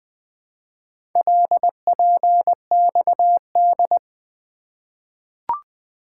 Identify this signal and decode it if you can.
Morse code, heard as a keyed tone, about 20 wpm, 700 Hz. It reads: LPXD
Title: Four Letter-Number Combinations - 20wpm